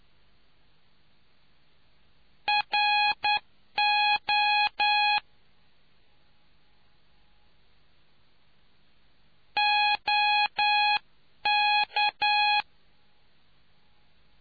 Morse code, heard as a keyed tone, about 9 wpm, 830 Hz. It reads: RO OK